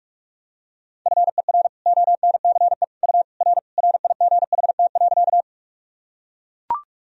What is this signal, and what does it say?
Morse code, heard as a keyed tone, about 41 wpm, 700 Hz. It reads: FER ONCE U R RIGHT.
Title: Sentences from Top 300 Words and Abbreviations 45wpm